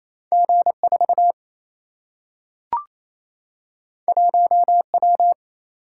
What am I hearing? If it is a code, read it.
Morse code, heard as a keyed tone, about 28 wpm, 700 Hz. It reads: G4 1W
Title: Single-Letter Single-Number 28wpm